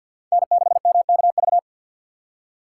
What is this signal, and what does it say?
Morse code, heard as a keyed tone, about 63 wpm, 700 Hz. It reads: N6MKV